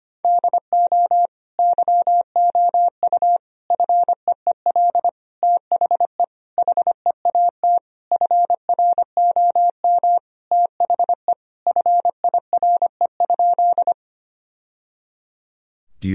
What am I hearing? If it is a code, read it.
Morse code, heard as a keyed tone, about 25 wpm, 700 Hz. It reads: DO YOU FEEL THE HEAT FROM THE FIRE?
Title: Sentences from Top 1000 Words - Review Part 3 - 25wpm